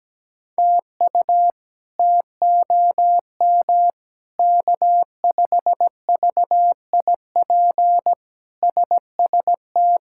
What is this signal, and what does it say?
Morse code, heard as a keyed tone, about 17 wpm, 700 Hz. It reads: TU TOM K5VIP SST
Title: K1USN SST QSO Practice - 17wpm